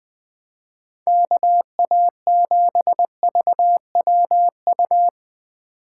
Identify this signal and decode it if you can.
Morse code, heard as a keyed tone, about 24 wpm, 700 Hz. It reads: KA7VWU
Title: Speed Racing - USA 2x3 Call Signs 20wpm